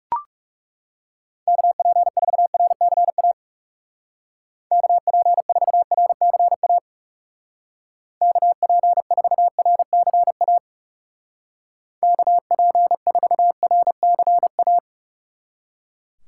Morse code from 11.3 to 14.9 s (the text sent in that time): KP4RCA